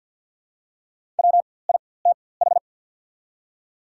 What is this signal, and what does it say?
Morse code, heard as a keyed tone, about 48 wpm, spaced as for 23 wpm, 700 Hz. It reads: WITH
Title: 4x Character Spacing - Top 300 Words - 50wpm